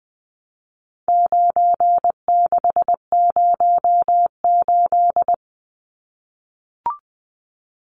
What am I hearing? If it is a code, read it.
Morse code, heard as a keyed tone, about 20 wpm, 700 Hz. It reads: 9608
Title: Numbers - 4-Digit 20wpm